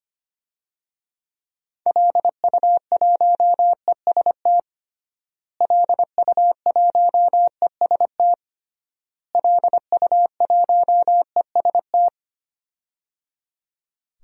Morse code, heard as a keyed tone, about 25 wpm, 700 Hz. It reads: LU1EST LU1EST LU1EST
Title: DXCC-100 Call Signs - 6 Characters - Repeated 3x - 25wpm